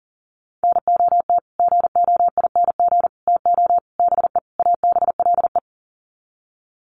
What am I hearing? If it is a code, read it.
Morse code, heard as a keyed tone, about 38 wpm, 700 Hz. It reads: NOT GOING TO BE ABLE
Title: Sets of 5 Words 40wpm